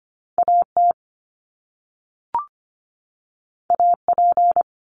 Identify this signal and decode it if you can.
Morse code, heard as a keyed tone, about 25 wpm, 700 Hz. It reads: AT AP
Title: Top Two Letter Combinations in Words 25wpm